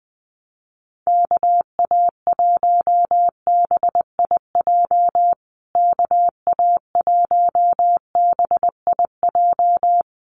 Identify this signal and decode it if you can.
Morse code, heard as a keyed tone, about 19 wpm, 700 Hz. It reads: KA1BIJ KA1BIJ
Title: USA Call Signs Repeated 2x 20wpm